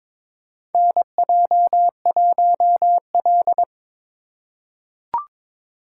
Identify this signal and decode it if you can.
Morse code, heard as a keyed tone, about 22 wpm, 700 Hz. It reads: NJ1L
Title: USA Call Signs - 4 Characters - 4x Character Spacing - 22wpm